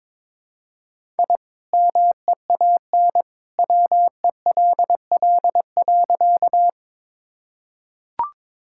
Morse code, heard as a keyed tone, about 23 wpm, 700 Hz. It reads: I MEAN WELL.
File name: practice / 22wpm / Sentences with 2-Word Phrases - 22wpm